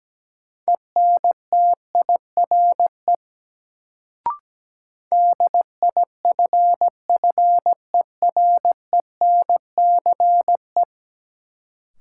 Morse code, heard as a keyed tone, about 17 wpm, 700 Hz. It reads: ENTIRE DIFFERENCE
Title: Sentences from Top 800 Words - New Word Drill - 17wpm